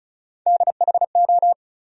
Morse code, heard as a keyed tone, about 36 wpm, 700 Hz. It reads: DHO
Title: USA Call Sign Postfixes - 3 Letters 35wpm